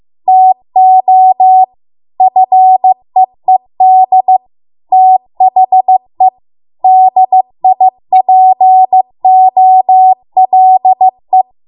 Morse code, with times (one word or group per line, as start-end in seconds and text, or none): 0.3-1.6 TO
2.2-4.4 FEED
4.9-6.3 THE
6.8-11.4 DIPOLE